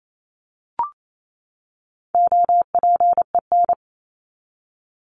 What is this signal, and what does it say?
Morse code, heard as a keyed tone, about 28 wpm, 700 Hz. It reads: OPEN
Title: Top 300 Words 28wpm